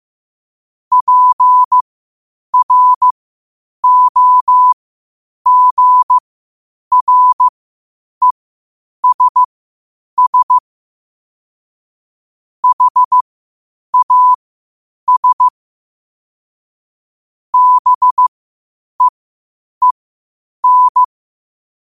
Morse code, heard as a keyed tone, about 15 wpm, spaced as for 8 wpm, 1000 Hz. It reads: PROGRESS HAS BEEN
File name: qotdfiles / 8wpm